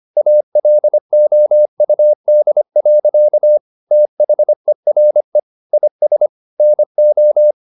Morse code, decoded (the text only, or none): ALOUD. THERE IS NO